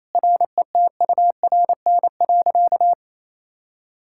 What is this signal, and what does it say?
Morse code, heard as a keyed tone, about 28 wpm, 700 Hz. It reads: RETURN.
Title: Sentences from Top 1000 Words - Review Part 6 - 28wpm